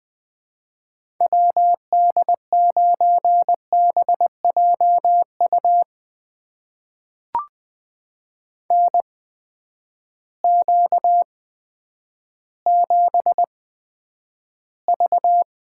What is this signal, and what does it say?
Morse code, heard as a keyed tone, about 20 wpm, spaced as for 10 wpm, 700 Hz. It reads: WD9BJU NQ7V
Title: USA Call Signs - 8x Character Spacing - 20wpm